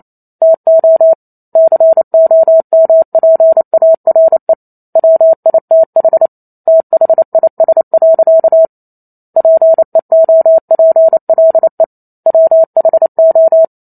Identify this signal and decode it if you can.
Morse code, heard as a keyed tone, about 28 wpm, 650 Hz. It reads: TO COMPARE WITH THIS. PEOPLE WHO